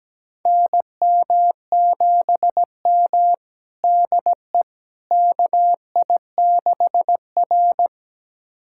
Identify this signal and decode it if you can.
Morse code, heard as a keyed tone, about 17 wpm, 700 Hz. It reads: NM7M DE KI6R
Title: USA Callsign DE Callsign - 4 Characters - 17wpm